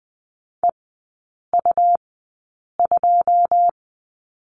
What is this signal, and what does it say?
Morse code, heard as a keyed tone, about 20 wpm, spaced as for 8 wpm, 700 Hz. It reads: EU2
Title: 205 Lesson 33e - Letter j - Three Characters 20wpm